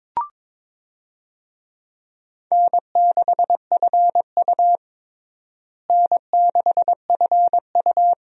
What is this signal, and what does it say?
Morse code, heard as a keyed tone, about 22 wpm, 700 Hz. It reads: N6FU N6FU
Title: USA Call Signs - 4 Characters - Repeated 3x - 22wpm